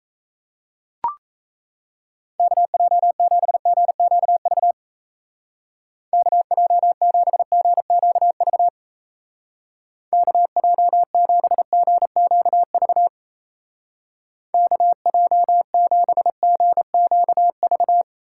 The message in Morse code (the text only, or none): KJ7GQV KJ7GQV KJ7GQV KJ7GQV